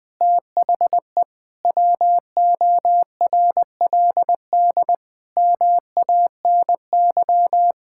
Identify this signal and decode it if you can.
Morse code, heard as a keyed tone, about 20 wpm, 700 Hz. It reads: THE WORLD MANY
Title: Sentences with 6-Word Phrases - Spoken Beginning - 20wpm